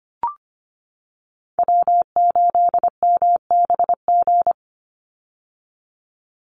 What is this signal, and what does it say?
Morse code, heard as a keyed tone, about 25 wpm, 700 Hz. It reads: W8MBG
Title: USA Call Signs 25wpm